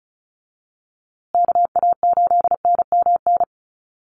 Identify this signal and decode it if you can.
Morse code, heard as a keyed tone, about 34 wpm, 700 Hz. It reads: KA8NMN
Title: USA Call Signs Repeated 3x 35wpm